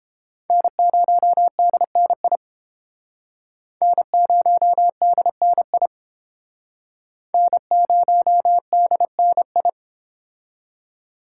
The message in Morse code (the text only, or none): N0DNI N0DNI N0DNI